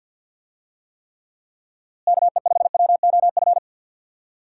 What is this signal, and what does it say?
Morse code, heard as a keyed tone, about 45 wpm, 700 Hz. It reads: KE5WOF
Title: USA Call Signs - 6 Characters - Repeated 3x - 50wpm